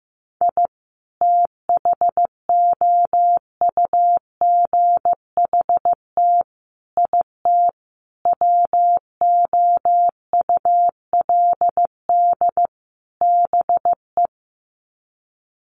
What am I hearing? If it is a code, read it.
Morse code, heard as a keyed tone, about 15 wpm, 700 Hz. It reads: I THOUGHT IT WOULD BE
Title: Sets of 5 Words Encore 15wpm